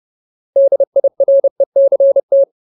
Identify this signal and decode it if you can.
Morse code, heard as a keyed tone, about 30 wpm, 550 Hz. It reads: DIRECT
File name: Ultimate Word List / 30WPM Ultimate